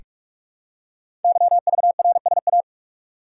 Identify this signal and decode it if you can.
Morse code, heard as a keyed tone, about 44 wpm, 700 Hz. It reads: YVRSA
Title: Code Groups - 5 Letters - 45wpm